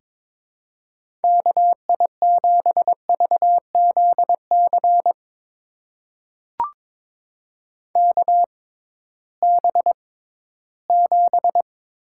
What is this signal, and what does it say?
Morse code, heard as a keyed tone, about 22 wpm, spaced as for 14 wpm, 700 Hz. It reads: KI7VZC KB7
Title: USA Call Signs - 6 Characters - 6x Character Spacing - 22wpm